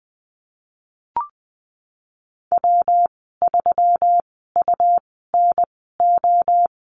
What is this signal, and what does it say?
Morse code, heard as a keyed tone, about 20 wpm, spaced as for 15 wpm, 700 Hz. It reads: W3UNO
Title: 179 Lesson 29g - Slash - Call Signs 20wpm